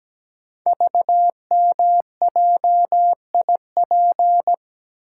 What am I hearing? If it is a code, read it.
Morse code, heard as a keyed tone, about 17 wpm, 700 Hz. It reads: VMJIP
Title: Code Groups - 5 Letter:Numbers - 17wpm